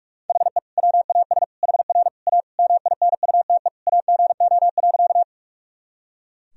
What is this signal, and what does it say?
Morse code, heard as a keyed tone, about 42 wpm, 700 Hz. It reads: HE WAS HR A MINUTE AGO.
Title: Sentences from Top 500 Words and Abbreviations 45wpm